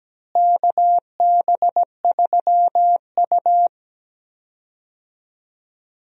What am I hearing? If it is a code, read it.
Morse code, heard as a keyed tone, about 17 wpm, 700 Hz. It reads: KB3U
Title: USA Call Signs - 4 Characters - Repeated 2x - 17wpm